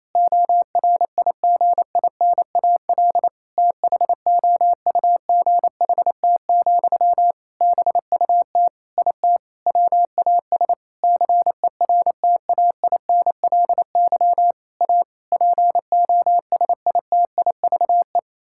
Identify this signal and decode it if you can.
Morse code, heard as a keyed tone, about 28 wpm, 700 Hz. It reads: ORIGINAL THOUGHT, BUT IT WAS CERTAINLY A POSITIVE
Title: Sentences from Top 1000 Words - Review Part 6 - 28wpm